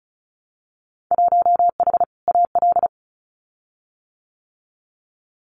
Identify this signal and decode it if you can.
Morse code, heard as a keyed tone, about 36 wpm, 700 Hz. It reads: 1H AL
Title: Winter Field Day Exchanges 35wpm